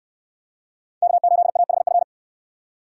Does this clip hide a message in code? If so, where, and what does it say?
Yes: Morse code from 1.0 to 2.0 s, heard as a keyed tone, about 69 wpm, 700 Hz. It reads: K8AHF